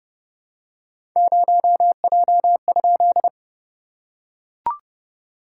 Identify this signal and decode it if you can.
Morse code, heard as a keyed tone, about 30 wpm, 690 Hz. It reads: 0J?
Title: SOTA QSO Practice 30wpm